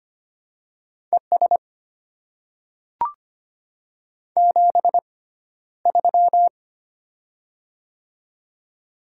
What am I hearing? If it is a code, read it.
Morse code, heard as a keyed tone, about 25 wpm, spaced as for 13 wpm, 700 Hz. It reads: ES 73
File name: Top 90 QSO Elements - 6x Character Spacing - 25wpm